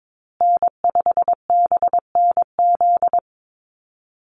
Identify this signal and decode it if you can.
Morse code, heard as a keyed tone, about 22 wpm, 700 Hz. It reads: N5BNZ
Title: USA Call Signs Repeated 3x 22wpm